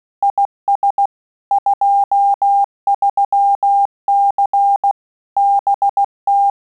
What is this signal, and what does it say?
Morse code, heard as a keyed tone, about 16 wpm, 800 Hz. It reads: IS 23C BT
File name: QSO CW 75 CPM